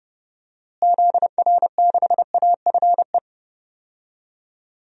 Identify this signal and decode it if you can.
Morse code, heard as a keyed tone, about 30 wpm, 700 Hz. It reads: ZR6AFE
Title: DXCC-100 Call Signs - 6 Characters - Repeated 2x - 30wpm